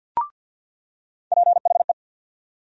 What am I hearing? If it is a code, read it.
Morse code, heard as a keyed tone, about 53 wpm, 700 Hz. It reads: PHE
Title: USA Call Sign Postfixes - 3 Letters 50wpm